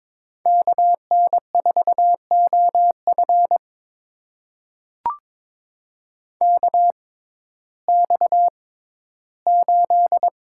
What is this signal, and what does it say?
Morse code, heard as a keyed tone, about 22 wpm, spaced as for 13 wpm, 700 Hz. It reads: KN4OF KX8
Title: USA Call Signs - 6x Character Spacing - 22wpm